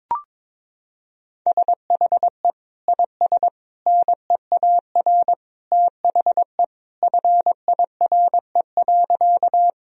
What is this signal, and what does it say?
Morse code, heard as a keyed tone, about 22 wpm, 700 Hz. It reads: SHE IS NEAR THE FIRE.
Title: Sentences from Top 1000 Words - Review Part 1 - 22wpm